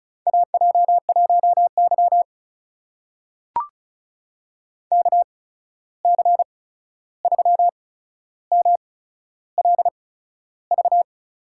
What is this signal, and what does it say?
Morse code, heard as a keyed tone, about 36 wpm, spaced as for 13 wpm, 700 Hz. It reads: AJ1Y KC3MLV
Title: USA Call Signs - 8x Character Spacing - 35wpm